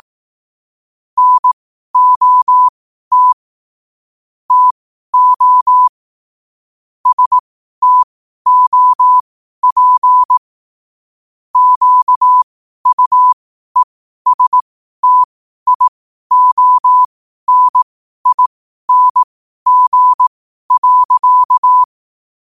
Morse code, heard as a keyed tone, about 18 wpm, spaced as for 12 wpm, 1000 Hz. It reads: NOT TO STOP QUESTIONING.
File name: qotdfiles / 12wpm